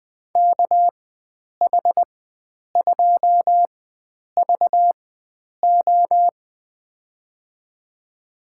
Morse code, heard as a keyed tone, about 20 wpm, spaced as for 9 wpm, 700 Hz. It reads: KH2VO
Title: USA Call Signs - 4x Character Spacing - 20wpm